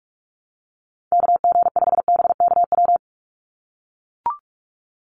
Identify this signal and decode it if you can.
Morse code, heard as a keyed tone, about 44 wpm, 700 Hz. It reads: KG5BKW
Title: USA Call Signs 45wpm